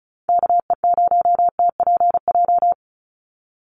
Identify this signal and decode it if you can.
Morse code, heard as a keyed tone, about 34 wpm, 700 Hz. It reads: KE0TPJ
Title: USA Call Signs 35wpm